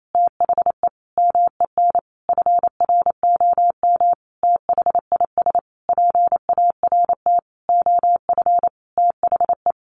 Morse code, heard as a keyed tone, about 28 wpm, 700 Hz. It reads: THE MEN FROM THIS PART OF THE
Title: Sentences from Top 300 Words 28wpm